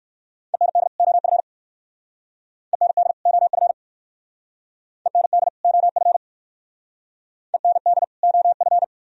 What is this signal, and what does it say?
Morse code, heard as a keyed tone, about 56 wpm, 700 Hz. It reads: END OF END OF END OF END OF